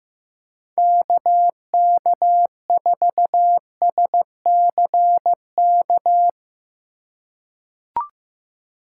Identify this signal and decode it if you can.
Morse code, heard as a keyed tone, about 16 wpm, 700 Hz. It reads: KK4SCK
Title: USA Call Signs - 6 Characters - 6x Character Spacing - 15wpm